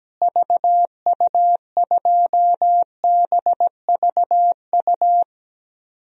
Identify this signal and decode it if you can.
Morse code, heard as a keyed tone, about 17 wpm, 700 Hz. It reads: VU2BVU
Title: DXCC-100 Call Signs - 6 Characters - Repeated 3x - 17wpm